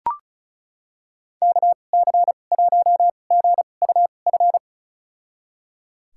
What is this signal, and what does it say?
Morse code, heard as a keyed tone, about 36 wpm, spaced as for 25 wpm, 700 Hz. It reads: KC1GUF
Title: USA Call Signs - 6 Characters - 2x Character Spacing - 35wpm